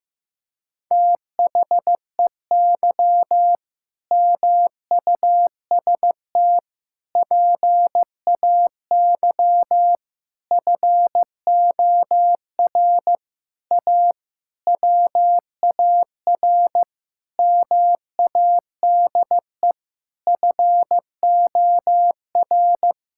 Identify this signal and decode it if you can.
Morse code, heard as a keyed tone, about 16 wpm, 700 Hz. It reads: THEY MUST PAY FOR A WAR MADE FOR
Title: Sentences from Top 1000 Words - Review Part 6 - 15wpm